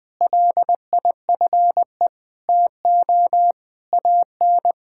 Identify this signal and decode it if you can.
Morse code, heard as a keyed tone, about 20 wpm, 700 Hz. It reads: LIFE TO AN
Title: Sentences from Top 1000 Words - Review Part 5 - 20wpm